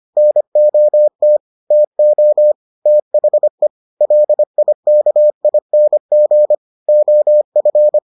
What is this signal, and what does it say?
Morse code, heard as a keyed tone, about 25 wpm, 600 Hz. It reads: NOT TO THE LIKING OF